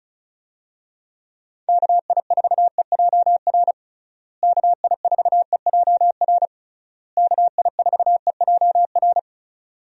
Morse code, heard as a keyed tone, about 34 wpm, 700 Hz. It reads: KI4EJR KI4EJR KI4EJR
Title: USA Call Signs - 6 Characters - Repeated 3x - 35wpm